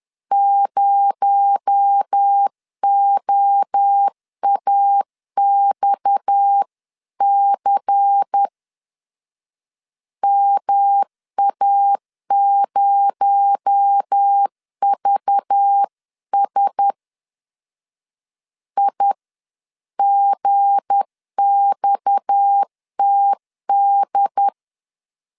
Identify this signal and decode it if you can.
Morse code, heard as a keyed tone, about 10 wpm, 800 Hz. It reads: 0OAXC MA0VS IGXTD
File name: LEZIONI / 7 GF9Z / 7 G 10wpm 2501